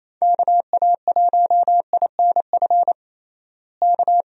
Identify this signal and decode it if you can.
Morse code, heard as a keyed tone, about 27 wpm, 700 Hz. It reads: KA1INF K
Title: USA Call Signs - 6 Characters - Repeated 3x - 28wpm